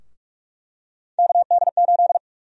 Morse code, heard as a keyed tone, about 43 wpm, 700 Hz. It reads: KD8
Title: USA Call Sign Prefixes - 45wpm